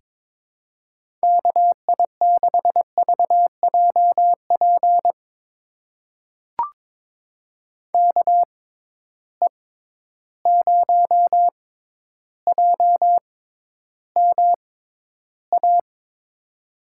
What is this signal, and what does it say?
Morse code, heard as a keyed tone, about 22 wpm, spaced as for 11 wpm, 700 Hz. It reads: KI6VJP KE0JMA
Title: USA Call Signs - 6 Characters - 6x Character Spacing - 22wpm